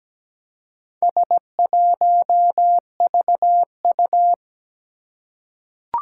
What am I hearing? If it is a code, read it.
Morse code, heard as a keyed tone, about 17 wpm, 700 Hz. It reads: S1VU
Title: Four Challenging Letter-Number Combinations 17wpm